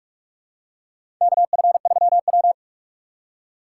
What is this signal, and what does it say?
Morse code, heard as a keyed tone, about 45 wpm, 700 Hz. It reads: KF3W
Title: USA Call Signs - 4 Characters - 4x Character Spacing - 45wpm